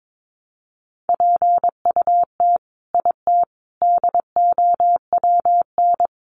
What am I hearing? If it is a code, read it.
Morse code, heard as a keyed tone, about 22 wpm, 700 Hz. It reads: PUT IT DOWN
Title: Sentences from Top 1400 Words 22wpm